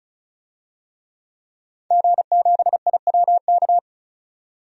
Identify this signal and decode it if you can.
Morse code, heard as a keyed tone, about 34 wpm, 700 Hz. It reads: G7IWK
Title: DXCC-100 Call Signs - 5 Characters - 35wpm